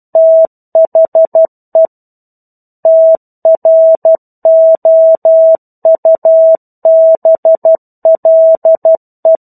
Morse code, heard as a keyed tone, about 12 wpm, 650 Hz. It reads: THE TROUBLE
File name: 01-Chapter-01 12 WPM